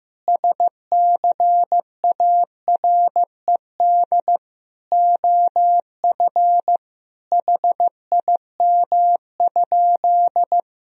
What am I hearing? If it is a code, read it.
Morse code, heard as a keyed tone, about 15 wpm, 700 Hz. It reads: SCARED OF HIM?
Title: Sentences from Top 1000 Words - Review Part 4 - 15wpm